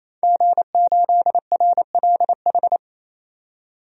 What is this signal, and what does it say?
Morse code, heard as a keyed tone, about 28 wpm, 700 Hz. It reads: G8RLH
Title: DXCC-100 Call Signs - 5 Characters - 28wpm